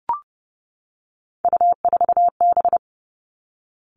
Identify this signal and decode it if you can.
Morse code, heard as a keyed tone, about 30 wpm, 700 Hz. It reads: U4B
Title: Three Challenging Letter-Number Combinations 30wpm